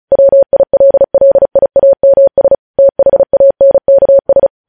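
Morse code, heard as a keyed tone, about 36 wpm, 550 Hz. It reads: WILLIAMS THANKS
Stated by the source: Morse code transmission of news headlines, preceded by vvv.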